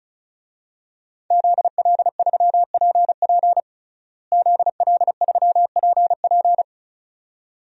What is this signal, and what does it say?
Morse code, heard as a keyed tone, about 34 wpm, 700 Hz. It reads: ZL3PP ZL3PP
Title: DXCC-100 Call Signs - 5 Characters - Repeated 2x - 35wpm